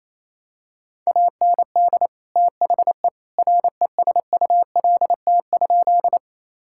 Morse code, heard as a keyed tone, about 28 wpm, 700 Hz. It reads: AND THE RESULT?
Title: Sentences from Top 1000 Words - Review Part 5 - 28wpm